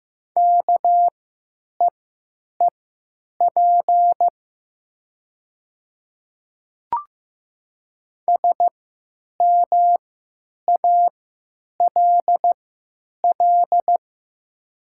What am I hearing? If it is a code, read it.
Morse code, heard as a keyed tone, about 15 wpm, spaced as for 9 wpm, 700 Hz. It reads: KEEP SMALL
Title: 3x Character Spacing - Top 300 Words - 15wpm